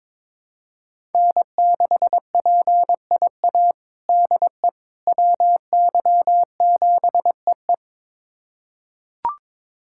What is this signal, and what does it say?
Morse code, heard as a keyed tone, about 22 wpm, 700 Hz. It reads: N6PIA DE WY7EE
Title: USA Callsign DE Callsign - 5 Characters - 22wpm